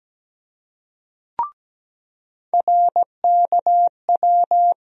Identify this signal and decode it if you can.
Morse code, heard as a keyed tone, about 17 wpm, 700 Hz. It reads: RKW
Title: USA Call Sign Postfixes - 3 Letters 17wpm